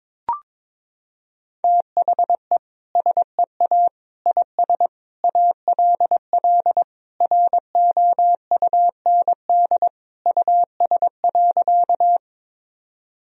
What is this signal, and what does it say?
Morse code, heard as a keyed tone, about 22 wpm, 700 Hz. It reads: THE SEA IS ALL ROUND US.
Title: Sentences from Top 1000 Words - Review Part 1 - 22wpm